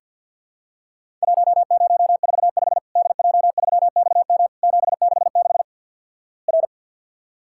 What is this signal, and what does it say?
Morse code, heard as a keyed tone, about 50 wpm, 700 Hz. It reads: 1045 DJ3XM 766 R
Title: Worked All Europe DX Contest - QTC Exchange - 50wpm